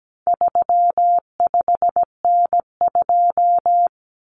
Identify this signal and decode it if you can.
Morse code, heard as a keyed tone, about 17 wpm, 700 Hz. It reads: 35N2
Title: Common Cut Numbers - 4-Digit 17wpm